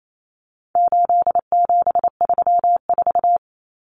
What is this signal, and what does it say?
Morse code, heard as a keyed tone, about 28 wpm, 700 Hz. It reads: 8734
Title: Common Cut Numbers - 4-Digit 28wpm